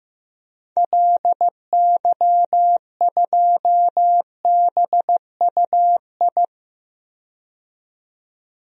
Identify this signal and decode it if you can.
Morse code, heard as a keyed tone, about 15 wpm, 700 Hz. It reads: LY2BUI
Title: DXCC-100 Call Signs - 6 Characters - 15wpm